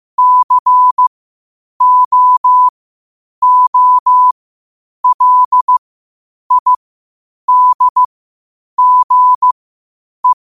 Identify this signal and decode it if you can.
Morse code, heard as a keyed tone, about 15 wpm, spaced as for 8 wpm, 1000 Hz. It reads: COOLIDGE
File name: qotdfiles / 8wpm